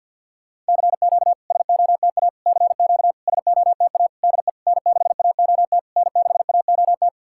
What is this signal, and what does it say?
Morse code, heard as a keyed tone, about 50 wpm, 700 Hz. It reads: CQ SOTA CQ SOTA DE N6AOT N6AOT
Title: SOTA QSO Practice 50wpm